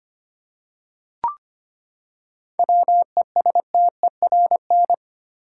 Morse code, heard as a keyed tone, about 25 wpm, 700 Hz. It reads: WESTERN
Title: Seven Letter Words - 25wpm